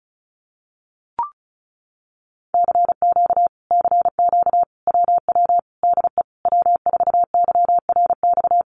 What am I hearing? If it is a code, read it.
Morse code, heard as a keyed tone, about 34 wpm, 700 Hz. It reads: CQ CQ WW DE W4YRX
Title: Common QSO Phrases Encore 35wpm